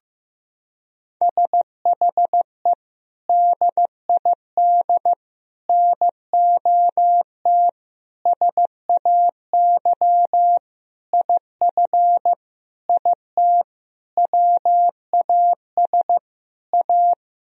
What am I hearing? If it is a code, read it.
Morse code, heard as a keyed tone, about 15 wpm, 700 Hz. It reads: SHE DID NOT SAY IF IT WAS A
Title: Sentences from Top 1000 Words - Review Part 3 - 15wpm